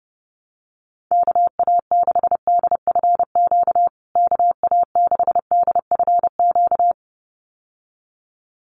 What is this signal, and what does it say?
Morse code, heard as a keyed tone, about 30 wpm, 700 Hz. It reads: KA6DFQ KA6DFQ
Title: USA Call Signs Repeated 2x 30wpm